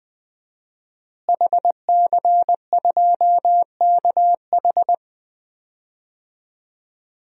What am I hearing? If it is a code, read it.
Morse code, heard as a keyed tone, about 21 wpm, 700 Hz. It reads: HC2KH
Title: DXCC-100 Call Signs - 5 Characters - Repeated 2x - 20wpm